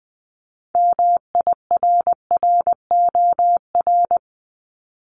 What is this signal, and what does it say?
Morse code, heard as a keyed tone, about 20 wpm, 700 Hz. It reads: MIRROR
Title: Multi-letter Phonemes 20wpm